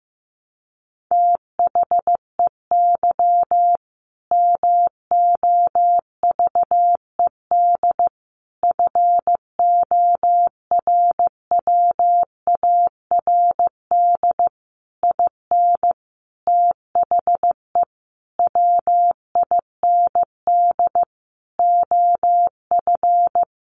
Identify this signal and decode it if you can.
Morse code, heard as a keyed tone, about 15 wpm, 700 Hz. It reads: THEY MOVED FORWARD IN THE WIND OF
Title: A Sound Of Thunder 15wpm